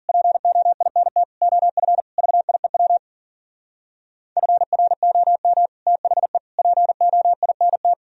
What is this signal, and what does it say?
Morse code, heard as a keyed tone, about 53 wpm, 700 Hz. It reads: POINT OF VIEW FROM THE POINT